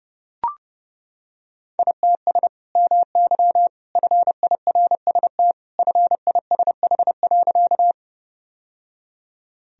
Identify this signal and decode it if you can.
Morse code, heard as a keyed tone, about 31 wpm, 700 Hz. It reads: ITS MY FIRST FISH.
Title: Sentences from Top 1000 Words - Review Part 1 - 30wpm